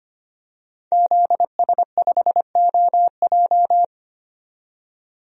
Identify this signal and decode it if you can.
Morse code, heard as a keyed tone, about 25 wpm, 700 Hz. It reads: ZS5OJ
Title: DXCC-100 Call Signs - 5 Characters - 25wpm